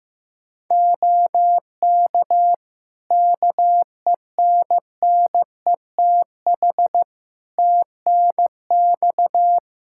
Morse code, heard as a keyed tone, about 15 wpm, 700 Hz. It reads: OK KENNETH TNX
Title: Practice QSO - Exchange 4 - Ending - 15wpm